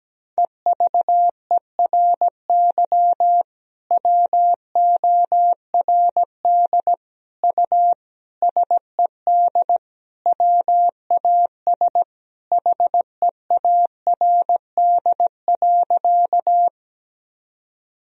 Morse code, heard as a keyed tone, about 17 wpm, 700 Hz. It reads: EVERY WORD U SED WAS HEARD.
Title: Sentences from Top 400 Words and Abbreviations 17wpm